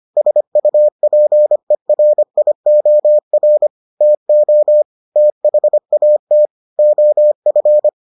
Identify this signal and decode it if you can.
Morse code, heard as a keyed tone, about 25 wpm, 600 Hz. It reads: SUPERIOR TO THAT OF